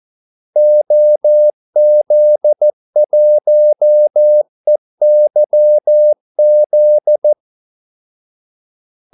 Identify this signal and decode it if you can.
Morse code, heard as a keyed tone, about 14 wpm, 600 Hz. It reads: OZ1EYZ
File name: Callsigns-14wpm